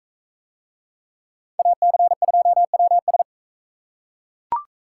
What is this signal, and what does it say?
Morse code, heard as a keyed tone, about 39 wpm, 700 Hz. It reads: AC2WS